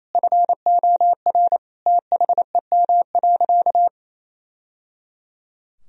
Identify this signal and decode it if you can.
Morse code, heard as a keyed tone, about 28 wpm, 700 Hz. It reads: FOR THEM.
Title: Sentences from Top 1000 Words - Review Part 4 - 28wpm